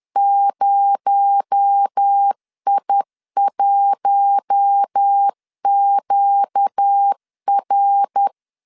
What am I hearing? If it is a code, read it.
Morse code, heard as a keyed tone, about 10 wpm, 800 Hz. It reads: 0I1QR